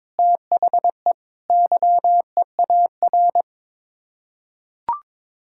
Morse code, heard as a keyed tone, about 22 wpm, 700 Hz. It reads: THE YEAR
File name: practice / 22wpm / Sets of 6 Words - 22wpm